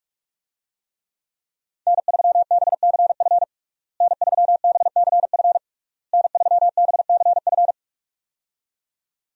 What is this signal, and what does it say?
Morse code, heard as a keyed tone, about 47 wpm, 700 Hz. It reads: N3BCF N3BCF N3BCF
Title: USA Call Signs - 5 Characters - Repeated 3x - 45wpm